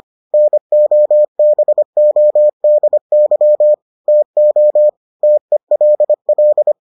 Morse code, heard as a keyed tone, about 25 wpm, 600 Hz. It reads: NOBODY TO TELL